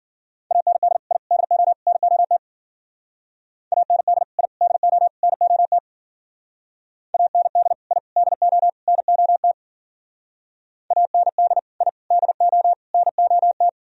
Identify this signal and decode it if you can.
Morse code, heard as a keyed tone, about 50 wpm, 700 Hz. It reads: AND I DO NOT AND I DO NOT AND I DO NOT AND I DO NOT
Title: Speed-Racing - Sets of 4 Words 40wpm